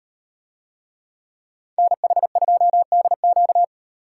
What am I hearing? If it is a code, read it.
Morse code, heard as a keyed tone, about 38 wpm, 700 Hz. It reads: NH2DQ